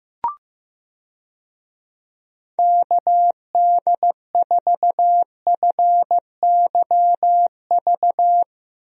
Morse code, heard as a keyed tone, about 15 wpm, 700 Hz. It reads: KD4FYV
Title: USA Call Signs - 6 Characters - Repeated 3x - 15wpm